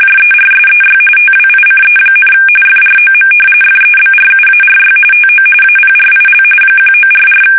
RTTY, about 45.45 baud. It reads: JERSEY TWICE, FROM FEBRUARY TO APRIL 1944
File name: fsk850 / battleshipnj